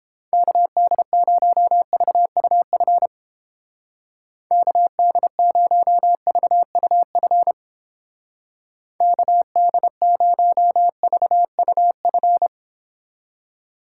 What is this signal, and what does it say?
Morse code, heard as a keyed tone, about 30 wpm, 700 Hz. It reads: KD0VUF KD0VUF KD0VUF